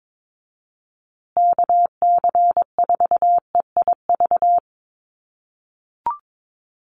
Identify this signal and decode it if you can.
Morse code, heard as a keyed tone, about 22 wpm, 700 Hz. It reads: KC4EIV
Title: USA Call Signs Repeated 3x 22wpm